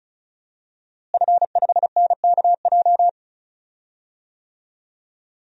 Morse code, heard as a keyed tone, about 36 wpm, 700 Hz. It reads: F5NKJ
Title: DXCC-100 Call Signs - 5 Characters - Repeated 3x - 35wpm